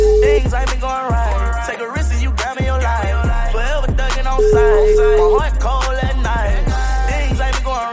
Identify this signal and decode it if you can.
Ringback tone (Europe, 425 Hz), with noise.